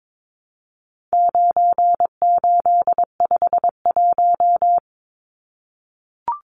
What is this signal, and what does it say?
Morse code, heard as a keyed tone, about 22 wpm, 700 Hz. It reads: 9851
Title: Numbers - 4-Digit 22wpm